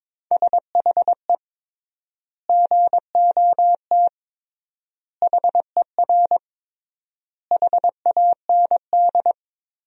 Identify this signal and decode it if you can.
Morse code, heard as a keyed tone, about 22 wpm, 700 Hz. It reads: SHE GOT HER HAND
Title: 3x Word-Spacing - Sentences from Top 500 Words - 22wpm